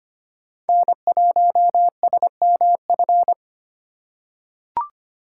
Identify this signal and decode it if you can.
Morse code, heard as a keyed tone, about 25 wpm, 700 Hz. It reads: N1SMF
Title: USA Call Signs - 5 Characters - 4x Character Spacing - 25wpm